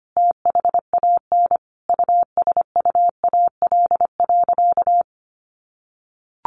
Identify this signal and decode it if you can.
Morse code, heard as a keyed tone, about 25 wpm, 700 Hz. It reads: THAN USUAL.
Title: Sentences from Top 1400 Words 25wpm